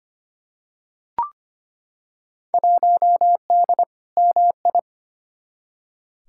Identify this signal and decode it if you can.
Morse code, heard as a keyed tone, about 25 wpm, 700 Hz. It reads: 1D MI
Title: Field Day Exchanges - 25wpm